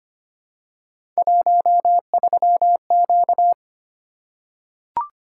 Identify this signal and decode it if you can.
Morse code, heard as a keyed tone, about 25 wpm, 700 Hz. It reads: 13Q
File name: Three Letter-Number Combinations - 25wpm